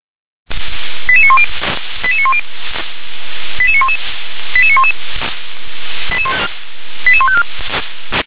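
Five-tone selective call (ZVEI), heard as quick runs of five tones, tones of about 70 ms. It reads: 88910 88910 88911 88911 88154
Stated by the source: ZVEI five-tone selective call received on SDR.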